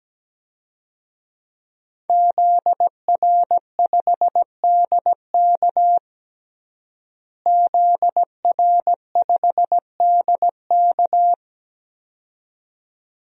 Morse code, heard as a keyed tone, about 17 wpm, 700 Hz. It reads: ZR5DK ZR5DK
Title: DXCC-100 Call Signs - 5 Characters - Repeated 2x - 17wpm